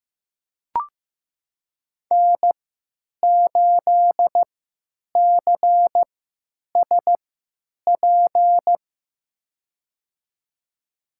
Morse code, heard as a keyed tone, about 15 wpm, spaced as for 9 wpm, 700 Hz. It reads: N8CSP
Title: USA Call Signs - 5 Characters - 3x Character Spacing - 15wpm